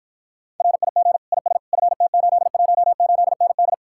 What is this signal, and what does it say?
Morse code, heard as a keyed tone, about 54 wpm, 700 Hz. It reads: RIG IS FT818ND